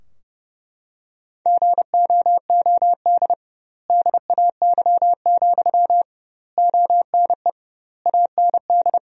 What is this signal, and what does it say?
Morse code, heard as a keyed tone, about 32 wpm, 700 Hz. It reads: GOOD DAY, ONE AND
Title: 2x Word-Spacing - Sentences from Top 200 Words - 30wpm